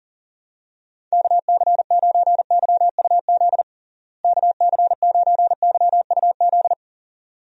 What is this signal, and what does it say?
Morse code, heard as a keyed tone, about 37 wpm, 700 Hz. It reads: KC9YUZ KC9YUZ
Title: USA Call Signs - 6 Characters - Repeated 2x - 40wpm